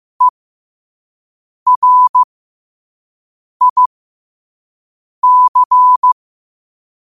Morse code, heard as a keyed tone, about 15 wpm, spaced as for 5 wpm, 1000 Hz. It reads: ERIC
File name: qotdfiles / 5wpm